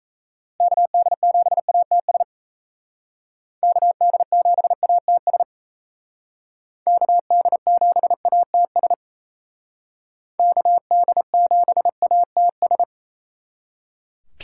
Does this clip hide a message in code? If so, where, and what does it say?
Yes: Morse code from 0.6 to 12.8 s, heard as a keyed tone, about 36 wpm, 700 Hz. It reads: KD7ATS KD7ATS KD7ATS KD7ATS